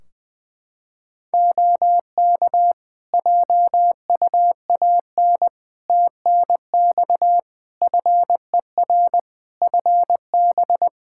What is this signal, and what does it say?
Morse code, heard as a keyed tone, about 20 wpm, 700 Hz. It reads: OK JUAN TNX FER FB
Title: Practice QSO - Exchange 4 - Ending - 20wpm